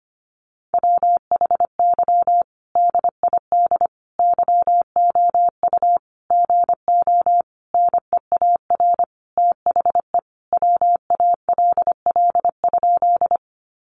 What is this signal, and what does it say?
Morse code, heard as a keyed tone, about 25 wpm, 700 Hz. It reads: WHY DID YOU GO NEAR THE WALL?
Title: Sentences from Top 1100 Words 25wpm